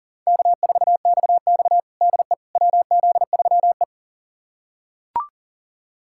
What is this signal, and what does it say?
Morse code, heard as a keyed tone, about 40 wpm, 700 Hz. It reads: K4XX DE WZ3E
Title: USA Callsign DE Callsign - 4 Characters - 40wpm